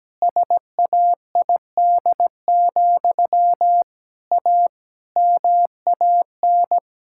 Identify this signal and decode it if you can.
Morse code, heard as a keyed tone, about 17 wpm, 700 Hz. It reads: SAID, A MAN
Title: Sentences from Top 1000 Words - Review Part 6 - 17wpm